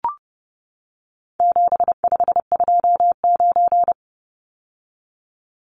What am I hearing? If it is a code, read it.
Morse code, heard as a keyed tone, about 30 wpm, 700 Hz. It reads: 7529
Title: Numbers - 4-Digit 30wpm